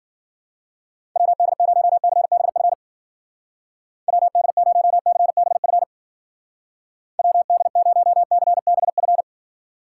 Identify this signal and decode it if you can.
Morse code, heard as a keyed tone, about 50 wpm, 700 Hz. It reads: WD0CBF WD0CBF WD0CBF